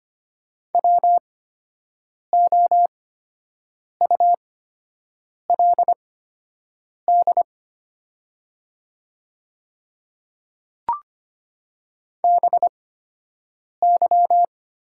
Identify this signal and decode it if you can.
Morse code, heard as a keyed tone, about 25 wpm, spaced as for 7 wpm, 700 Hz. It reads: WOULD BY
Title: 8x Character Spacing - Top 100 Words - 25wpm